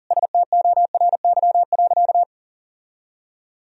Morse code, heard as a keyed tone, about 40 wpm, 700 Hz. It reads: STORY.
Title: Sentences from Top 1000 Words - Review Part 3 - 40wpm